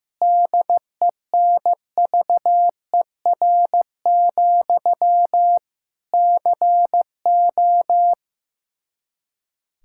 Morse code, heard as a keyed tone, about 15 wpm, 700 Hz. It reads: DENVER, CO
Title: K1USN SST QSO Practice 15wpm